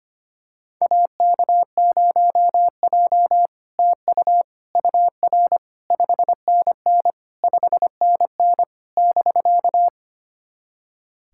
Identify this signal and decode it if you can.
Morse code, heard as a keyed tone, about 25 wpm, 690 Hz. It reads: AK0J TU UR 5NN 5NN <BK>
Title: SOTA QSO Practice 25wpm